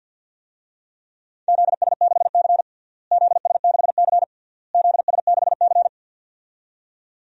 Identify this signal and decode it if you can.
Morse code, heard as a keyed tone, about 50 wpm, 700 Hz. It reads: ZS6C ZS6C ZS6C
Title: DXCC-100 Call Signs - 4 Characters - Repeated 3x - 50wpm